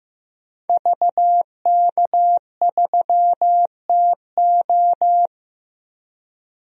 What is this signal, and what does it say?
Morse code, heard as a keyed tone, about 15 wpm, 700 Hz. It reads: VK3TO
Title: DXCC-100 Call Signs - 5 Characters - Repeated 3x - 15wpm